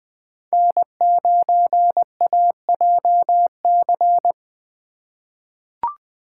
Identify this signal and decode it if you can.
Morse code, heard as a keyed tone, about 20 wpm, 700 Hz. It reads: N9AJC
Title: USA Call Signs - 6x Character Spacing - 20wpm